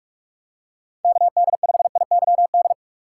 Morse code, heard as a keyed tone, about 44 wpm, 700 Hz. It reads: KD5IYD
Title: USA Call Signs - 6 Characters - 6x Character Spacing - 45wpm